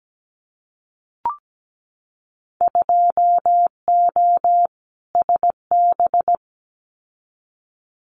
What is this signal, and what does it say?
Morse code, heard as a keyed tone, about 17 wpm, 700 Hz. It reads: 2O SB
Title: Winter Field Day Exchanges 17wpm